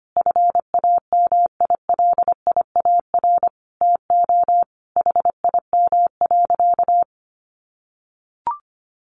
Morse code, heard as a keyed tone, about 25 wpm, 700 Hz. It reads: FAMILIAR TO HIM.
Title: Sentences from Top 1000 Words 25wpm